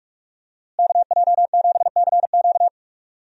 Morse code, heard as a keyed tone, about 41 wpm, 700 Hz. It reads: KJ7CQ
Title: USA Call Signs - 5 Characters - Repeated 3x - 45wpm